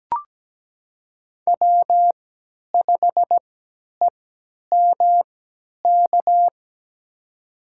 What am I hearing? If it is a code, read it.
Morse code, heard as a keyed tone, about 17 wpm, spaced as for 10 wpm, 700 Hz. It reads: W5EMK
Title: USA Call Signs - 5 Characters - 3x Character Spacing - 17wpm